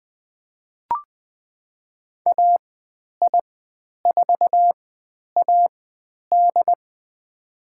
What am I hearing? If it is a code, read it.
Morse code, heard as a keyed tone, about 20 wpm, spaced as for 10 wpm, 700 Hz. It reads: AI4AD
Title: 102 Lesson 17g - Letter u - Call Signs 20wpm